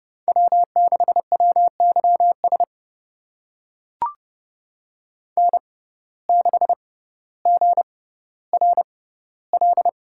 Morse code, heard as a keyed tone, about 29 wpm, spaced as for 15 wpm, 700 Hz. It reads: W6WYS N6GRL
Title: USA Call Signs - 5 Characters - 6x Character Spacing - 30wpm